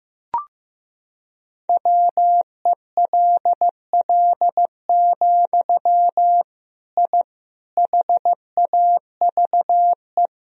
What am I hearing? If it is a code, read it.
Morse code, heard as a keyed tone, about 15 wpm, 700 Hz. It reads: WELL, I HAVE
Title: Sentences from Top 1000 Words - Review Part 3 - 15wpm